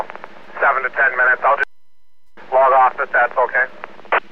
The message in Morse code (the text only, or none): none